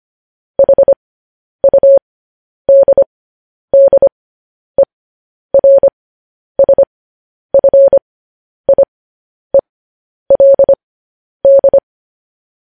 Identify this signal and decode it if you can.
Morse code, heard as a keyed tone, about 25 wpm, spaced as for 10 wpm, 550 Hz. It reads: HUDDERSFIELD
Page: news headlines in morse code at 10 wpm